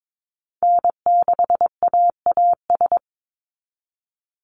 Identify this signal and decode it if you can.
Morse code, heard as a keyed tone, about 22 wpm, 700 Hz. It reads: N6AAS
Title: USA Call Signs 22wpm